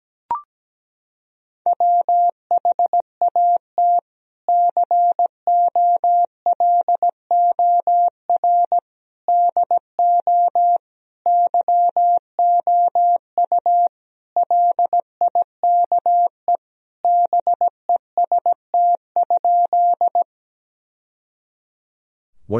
Morse code, heard as a keyed tone, about 17 wpm, 700 Hz. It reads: WHAT COLOR DO YOU LIKE BEST?
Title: Sentences from Top 1000 Words - Review Part 2 - 17wpm